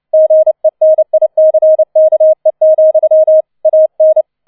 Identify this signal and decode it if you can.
Morse code, heard as a keyed tone, about 29 wpm, 630 Hz. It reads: GENICKE, AN